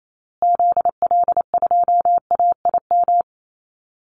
Morse code, heard as a keyed tone, about 28 wpm, 700 Hz. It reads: ZL2AIM
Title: Call Signs Repeated 2x - 28wpm